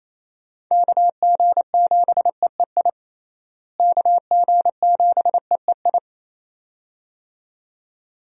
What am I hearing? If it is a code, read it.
Morse code, heard as a keyed tone, about 28 wpm, 700 Hz. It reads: KG7EEI KG7EEI
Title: USA Call Signs - 6 Characters - Repeated 2x - 28wpm